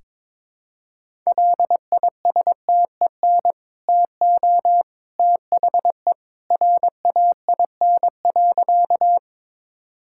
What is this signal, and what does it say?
Morse code, heard as a keyed tone, about 22 wpm, 690 Hz. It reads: LISTEN TO THE RAIN.
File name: practice / 22wpm / Easy Sentences with Hints - 22wpm